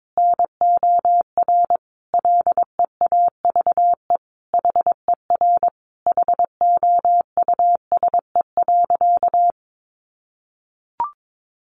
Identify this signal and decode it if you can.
Morse code, heard as a keyed tone, about 22 wpm, 700 Hz. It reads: NOR LEAVE HER HOUSE.
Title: Sentences from Top 1500 Words 22wpm